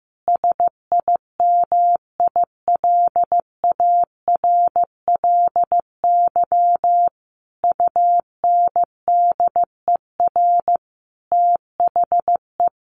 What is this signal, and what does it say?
Morse code, heard as a keyed tone, about 15 wpm, 700 Hz. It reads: SIMILARLY UNDER THE
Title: Sentences from Top 1400 Words 15wpm